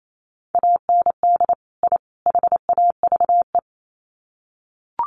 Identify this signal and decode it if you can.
Morse code, heard as a keyed tone, about 28 wpm, 700 Hz. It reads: AND I HAVE
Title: Sets of 3 Words 28wpm